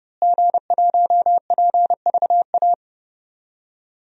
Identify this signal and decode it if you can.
Morse code, heard as a keyed tone, about 29 wpm, 700 Hz. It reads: G1PVA
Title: DXCC-100 Call Signs - 5 Characters - Repeated 2x - 30wpm